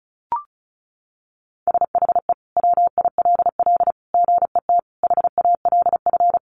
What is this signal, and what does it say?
Morse code, heard as a keyed tone, about 34 wpm, 700 Hz. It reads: SHE WILL GET HALF
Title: Sentences from Top 1200 Words 35wpm